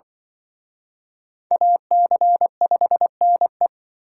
Morse code, heard as a keyed tone, about 24 wpm, 700 Hz. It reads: AC5NE
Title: Speed Racing - USA 2x2 Call Signs 20wpm